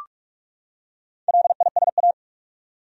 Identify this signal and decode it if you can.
Morse code, heard as a keyed tone, about 45 wpm, 700 Hz. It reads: LISA
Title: USA Common Names - 45wpm